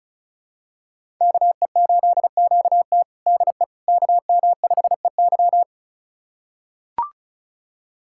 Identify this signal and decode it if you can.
Morse code, heard as a keyed tone, about 34 wpm, 700 Hz. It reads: KE8QT DE KM5EY
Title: USA Callsign DE Callsign - 5 Characters - 35wpm